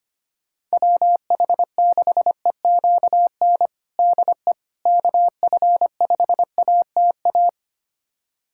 Morse code, heard as a keyed tone, about 25 wpm, 700 Hz. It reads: WH6EQN DE KF5ATA
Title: USA Callsign DE Callsign - 6 Characters - 25wpm